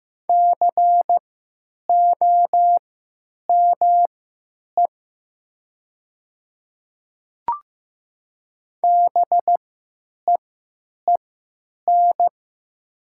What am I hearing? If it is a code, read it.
Morse code, heard as a keyed tone, about 15 wpm, spaced as for 8 wpm, 700 Hz. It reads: COME BEEN
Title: 3x Character Spacing - Top 100 Words - 15wpm